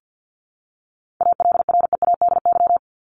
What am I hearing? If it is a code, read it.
Morse code, heard as a keyed tone, about 50 wpm, 700 Hz. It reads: ALREADY
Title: Top 501-750 Words 50wpm